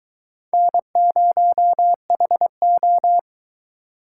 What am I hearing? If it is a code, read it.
Morse code, heard as a keyed tone, about 22 wpm, 700 Hz. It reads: N0HO